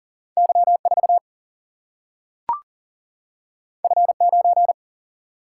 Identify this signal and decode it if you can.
Morse code, heard as a keyed tone, about 40 wpm, 700 Hz. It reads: Y4 F9
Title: Single-Letter Single-Number 40wpm